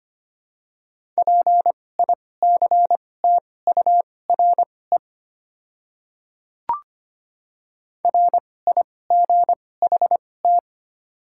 Morse code, heard as a keyed tone, about 25 wpm, spaced as for 18 wpm, 700 Hz. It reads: PICTURE RIGHT
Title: 2x Character Spacing - Top 200 Words - 25wpm